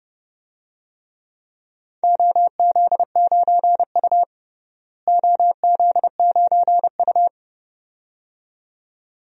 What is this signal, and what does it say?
Morse code, heard as a keyed tone, about 30 wpm, 700 Hz. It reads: OZ9U OZ9U
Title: DXCC-100 Call Signs - 4 Characters - Repeated 2x - 30wpm